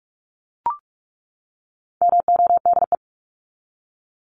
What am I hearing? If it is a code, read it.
Morse code, heard as a keyed tone, about 44 wpm, 700 Hz. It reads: MODE
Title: Top 1751-2000 Words 45wpm